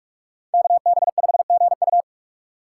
Morse code, heard as a keyed tone, about 43 wpm, 700 Hz. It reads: KB5GU
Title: USA Call Signs - 6x Character Spacing - 45wpm